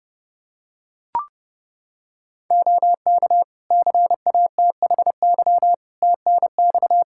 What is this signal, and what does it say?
Morse code, heard as a keyed tone, about 30 wpm, 700 Hz. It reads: OK CATHY TNX
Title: K1USN SST QSO Practice 30wpm